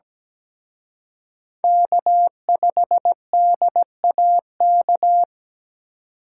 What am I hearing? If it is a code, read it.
Morse code, heard as a keyed tone, about 20 wpm, 700 Hz. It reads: K5DAK
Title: Speed Racing - USA 1x3 Call Signs 17wpm